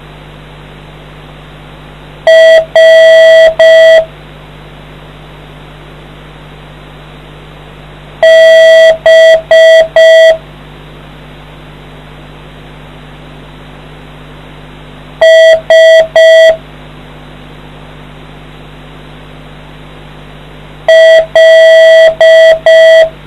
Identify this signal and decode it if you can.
Morse code, heard as a keyed tone, about 5 wpm, 650 Hz. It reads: R B S L